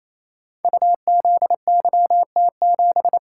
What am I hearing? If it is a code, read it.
Morse code, heard as a keyed tone, about 28 wpm, 690 Hz. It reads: UZYT7
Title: Code Groups - 5 Letter:Numbers - 28wpm